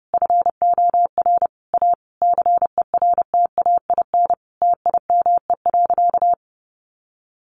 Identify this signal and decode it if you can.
Morse code, heard as a keyed tone, about 30 wpm, 700 Hz. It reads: FOR A CERTAIN TIME.
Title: Sentences from Top 1000 Words 30wpm